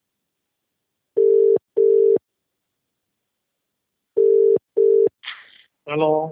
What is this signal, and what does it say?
Ringback tone (UK), 400 + 450 Hz, with speech.